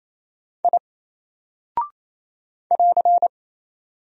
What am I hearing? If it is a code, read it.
Morse code, heard as a keyed tone, about 28 wpm, 700 Hz. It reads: I <AR>
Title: Single Character (Letter:Number:Punctuation:Prosign) Alt 28wpm